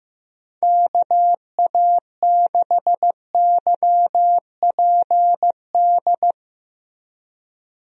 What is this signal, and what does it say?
Morse code, heard as a keyed tone, about 15 wpm, 700 Hz. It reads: KA6YPD
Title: USA Call Signs - 6 Characters - 15wpm